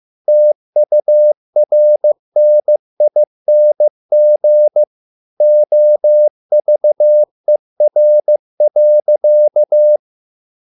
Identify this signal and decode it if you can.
Morse code, heard as a keyed tone, about 15 wpm, 600 Hz. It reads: TURNING OVER.